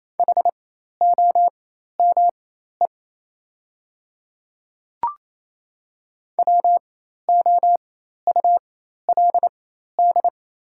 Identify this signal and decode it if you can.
Morse code, heard as a keyed tone, about 27 wpm, spaced as for 13 wpm, 690 Hz. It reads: HOME WOULD
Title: 4x Character Spacing - Top 300 Words - 28wpm